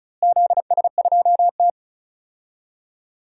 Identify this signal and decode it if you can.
Morse code, heard as a keyed tone, about 34 wpm, 700 Hz. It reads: ZS2T
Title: DXCC-100 Call Signs - 4 Characters - Repeated 2x - 35wpm